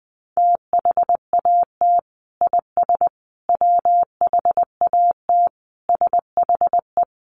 Morse code, heard as a keyed tone, about 20 wpm, 700 Hz. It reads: THAT IS WHAT SHE
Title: Sentences from Top 100 Words 20wpm